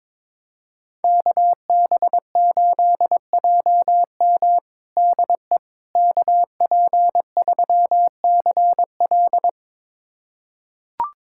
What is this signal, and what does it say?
Morse code, heard as a keyed tone, about 22 wpm, 700 Hz. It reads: KB8JM DE KP3CL
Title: USA Callsign DE Callsign - 5 Characters - 22wpm